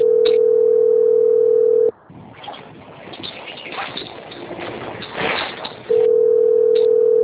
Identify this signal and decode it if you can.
Ringback tone (North America), 440 + 480 Hz.
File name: out- - - - - .3912